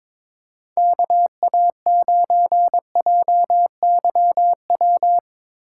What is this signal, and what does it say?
Morse code, heard as a keyed tone, about 21 wpm, 700 Hz. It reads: KA9JYW
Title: USA Call Signs - 4x Character Spacing - 22wpm